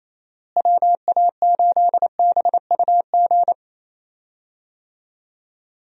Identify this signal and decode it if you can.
Morse code, heard as a keyed tone, about 27 wpm, 700 Hz. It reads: WA8BUG
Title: USA Call Signs - 6 Characters - Repeated 2x - 28wpm